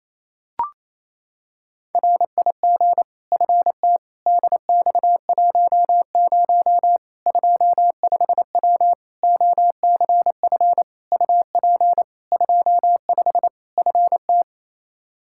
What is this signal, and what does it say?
Morse code, heard as a keyed tone, about 27 wpm, 700 Hz. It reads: RIG FT DX10 25W OCF UP 25 FT
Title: Practice QSO - Exchange 2 - Info - 28wpm